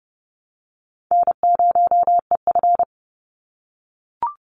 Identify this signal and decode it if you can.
Morse code, heard as a keyed tone, about 30 wpm, 700 Hz. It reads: N0EF
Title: Call Signs - 30wpm